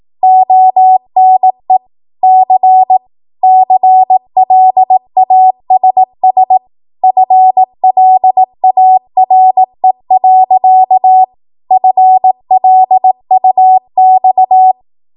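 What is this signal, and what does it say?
Morse code, heard as a keyed tone, about 18 wpm, 750 Hz. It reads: ONE C CLASS FLARE. FLUX
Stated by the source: Code practice at 18 WPM, a transition run bridging 15 to 20 WPM.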